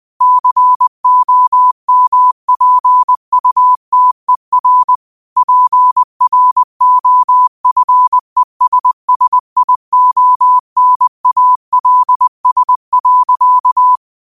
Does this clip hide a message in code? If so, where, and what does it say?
Yes: Morse code from 0.2 to 14.0 s, heard as a keyed tone, about 20 wpm, 1000 Hz. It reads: COMPUTER PROFESSIONALS.